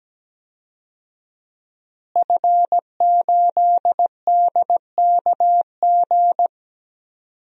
Morse code, heard as a keyed tone, about 17 wpm, 700 Hz. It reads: F8DKG
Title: DXCC-100 Call Signs - 5 Characters - Repeated 2x - 17wpm